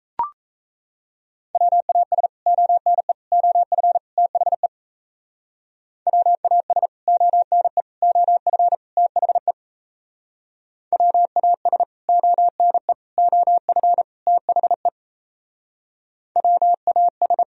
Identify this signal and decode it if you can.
Morse code, heard as a keyed tone, about 38 wpm, 700 Hz. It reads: WAS ONE OF THE WAS ONE OF THE WAS ONE OF THE WAS